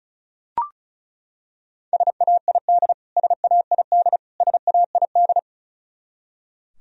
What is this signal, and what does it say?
Morse code, heard as a keyed tone, about 34 wpm, 700 Hz. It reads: SAID SAID SAID
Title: Top 100 Words Repeated 35wpm